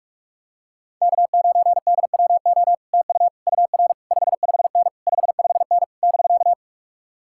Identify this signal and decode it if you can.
Morse code, heard as a keyed tone, about 51 wpm, 700 Hz. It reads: K9DWO TU UR 55N 55N <BK>